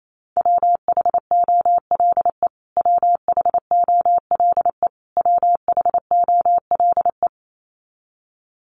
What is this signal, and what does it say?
Morse code, heard as a keyed tone, about 28 wpm, 700 Hz. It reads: WHOLE WHOLE WHOLE
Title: Top 500 Words Repeated 28wpm